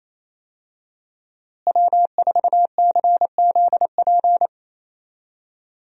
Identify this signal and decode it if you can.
Morse code, heard as a keyed tone, about 27 wpm, 700 Hz. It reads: W4CZP
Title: USA Call Signs - 5 Characters - Repeated 3x - 28wpm